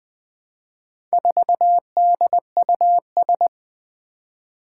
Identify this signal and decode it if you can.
Morse code, heard as a keyed tone, about 20 wpm, 700 Hz. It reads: 4DUS
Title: Four Challenging Letter-Number Combinations 20wpm